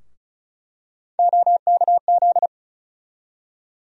Morse code, heard as a keyed tone, about 34 wpm, 700 Hz. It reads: OKZ
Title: USA Call Sign Postfixes - 3 Letters 35wpm